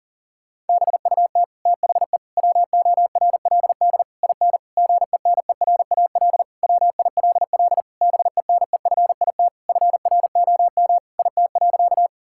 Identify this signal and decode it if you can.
Morse code, heard as a keyed tone, about 40 wpm, 700 Hz. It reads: BUT THE WORLD IN GENERAL WILL BENEFIT FROM IT.
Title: Sentences from Top 1000 Words - Review Part 6 - 40wpm